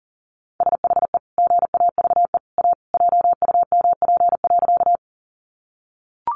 Morse code, heard as a keyed tone, about 40 wpm, 700 Hz. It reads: SHE GAVE A JUMP.
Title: Sentences from Top 1300 Words 40wpm